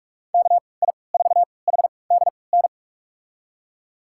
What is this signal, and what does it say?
Morse code, heard as a keyed tone, about 43 wpm, spaced as for 25 wpm, 700 Hz. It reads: KI4HDN
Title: USA Call Signs - 3x Character Spacing - 45wpm